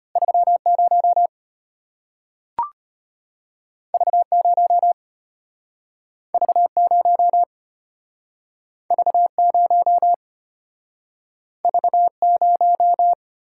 Morse code, heard as a keyed tone, about 34 wpm, 700 Hz. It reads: 30 V0 V0 V0 V0